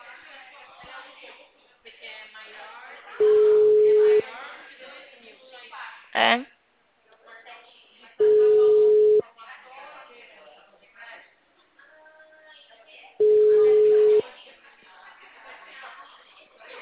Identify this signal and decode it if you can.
Ringback tone (Europe, 425 Hz), with speech.